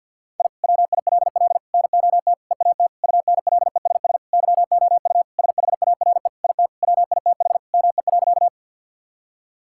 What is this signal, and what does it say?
Morse code, heard as a keyed tone, about 48 wpm, 700 Hz. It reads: I WILL NOT EAT UNLESS YOU SHARE IT WITH ME.
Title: Sentences from Top 1000 Words - Review Part 6 - 50wpm